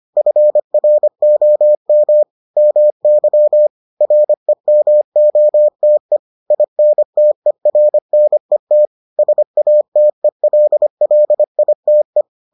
Morse code, heard as a keyed tone, about 25 wpm, 600 Hz. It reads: FROM MY REMOTE INTERNET SATELLITE